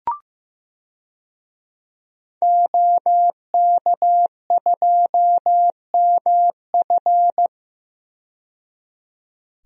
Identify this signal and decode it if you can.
Morse code, heard as a keyed tone, about 15 wpm, 700 Hz. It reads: OK2MF
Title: DXCC-100 Call Signs - 5 Characters - 15wpm